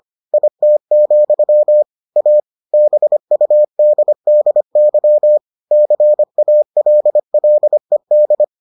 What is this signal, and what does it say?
Morse code, heard as a keyed tone, about 25 wpm, 600 Hz. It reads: IT, A BUDDY CALLED